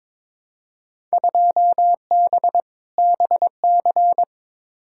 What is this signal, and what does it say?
Morse code, heard as a keyed tone, about 22 wpm, 700 Hz. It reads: 2B BC
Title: Field Day Exchanges Encore - 22wpm